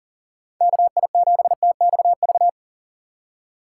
Morse code, heard as a keyed tone, about 39 wpm, 700 Hz. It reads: KI7TXV
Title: USA Call Signs - 4x Character Spacing - 40wpm